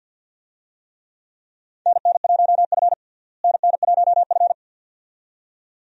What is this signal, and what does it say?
Morse code, heard as a keyed tone, about 49 wpm, 700 Hz. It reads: NN1F NN1F
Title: USA Call Signs - 4 Characters - Repeated 2x - 50wpm